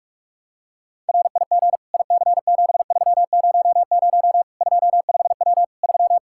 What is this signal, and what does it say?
Morse code, heard as a keyed tone, about 46 wpm, 700 Hz. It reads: RIG IC7300 25W 3
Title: Practice QSO - Exchange 2 - Info - 45wpm